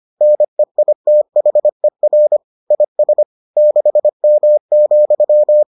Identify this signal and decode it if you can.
Morse code, heard as a keyed tone, about 25 wpm, 600 Hz. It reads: NEITHER IS 6M,